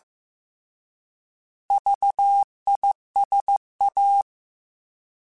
Morse code, heard as a keyed tone, about 15 wpm, 800 Hz. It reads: VISA
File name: Alphabetical / Alphabetical mp3s / Vs @13